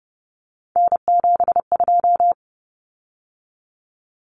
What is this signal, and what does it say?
Morse code, heard as a keyed tone, about 30 wpm, 700 Hz. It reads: N72
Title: Common Cut Numbers - 3-Digit 30wpm